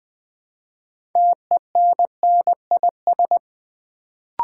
Morse code, heard as a keyed tone, about 20 wpm, 700 Hz. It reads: TENNIS
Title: Build A Word - 2 Letters At A Time - 6-Letter Words - 20wpm